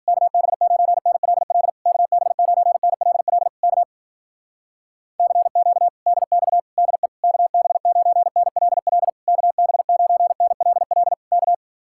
Morse code, heard as a keyed tone, about 56 wpm, 700 Hz. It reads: KB9NLL KB9NLL K CQ DX DE KB9NLL KB9NLL K